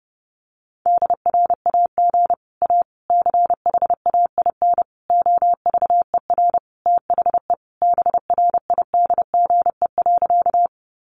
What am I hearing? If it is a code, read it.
Morse code, heard as a keyed tone, about 30 wpm, 700 Hz. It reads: DRAG A CHAIN OVER THE BRIDGE.
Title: Sentences from Top 1500 Words 30wpm